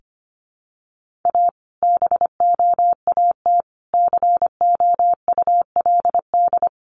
Morse code, heard as a keyed tone, about 25 wpm, 700 Hz. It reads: A BOAT COULD
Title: Sentences from Top 1400 Words 25wpm